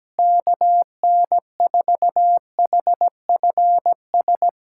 Morse code, heard as a keyed tone, about 17 wpm, 700 Hz. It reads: KN4HFS
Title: USA Callsign DE Callsign 17wpm